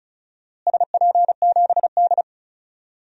Morse code, heard as a keyed tone, about 34 wpm, 700 Hz. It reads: SP7D
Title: DXCC-100 Call Signs - 4 Characters - Repeated 3x - 35wpm